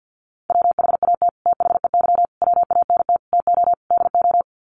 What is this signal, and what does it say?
Morse code, heard as a keyed tone, about 50 wpm, 700 Hz. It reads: WHAT THEY WANT TO DO
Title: Sets of 5 Words Encore2 50wpm